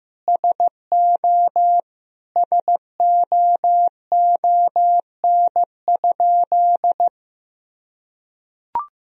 Morse code, heard as a keyed tone, about 15 wpm, 700 Hz. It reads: SO SOON?
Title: Sentences from Top 400 Words and Abbreviations 15wpm